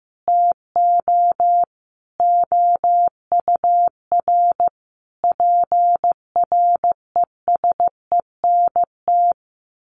Morse code, heard as a keyed tone, about 15 wpm, 700 Hz. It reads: TO OUR PRESENT
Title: Sentences from Top 1400 Words 15wpm